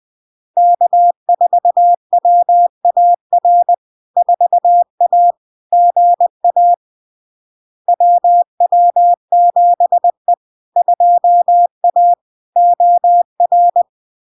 Morse code, heard as a keyed tone, about 20 wpm, 700 Hz. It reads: K4WAR 4A GA WW7E 2A OR
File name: LARK-FD-2022-Log-in-CW-20wpm-700-Hz